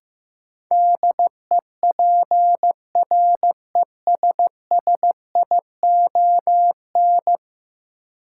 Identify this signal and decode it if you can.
Morse code, heard as a keyed tone, about 15 wpm, 700 Hz. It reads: DEPRESSION
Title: Sentences from Top 1300 Words - New Word Drill Alt - 15wpm